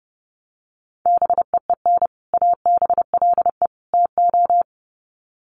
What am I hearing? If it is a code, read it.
Morse code, heard as a keyed tone, about 28 wpm, 700 Hz. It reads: BEEN ABLE TO
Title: Sets of 3 Words 30wpm